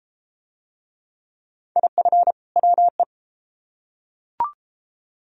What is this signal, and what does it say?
Morse code, heard as a keyed tone, about 29 wpm, 700 Hz. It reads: IF WE